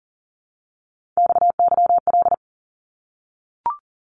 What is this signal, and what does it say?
Morse code, heard as a keyed tone, about 40 wpm, 700 Hz. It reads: XYL
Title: Abbreviations Repeated 3x - 40wpm